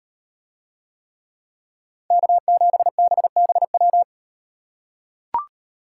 Morse code, heard as a keyed tone, about 38 wpm, 700 Hz. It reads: K7BBW